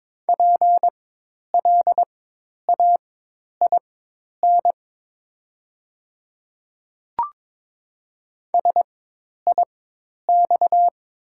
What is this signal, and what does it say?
Morse code, heard as a keyed tone, about 22 wpm, spaced as for 10 wpm, 700 Hz. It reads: PLAIN SIX
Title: 4x Character Spacing - Top 500 Words - 22wpm